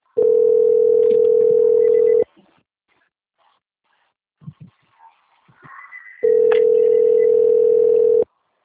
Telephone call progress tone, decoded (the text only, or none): ringback tone